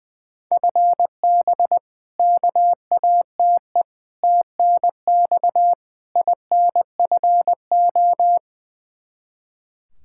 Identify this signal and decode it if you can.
Morse code, heard as a keyed tone, about 20 wpm, 700 Hz. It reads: FB KATE TNX INFO
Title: Practice QSO - Exchange 3 - More Info - 20wpm